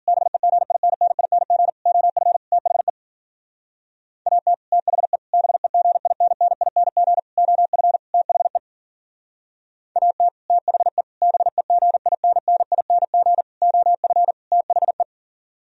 Morse code, heard as a keyed tone, about 52 wpm, 700 Hz. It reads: BEGINNING OF THE AT THE BEGINNING OF THE AT THE BEGINNING OF THE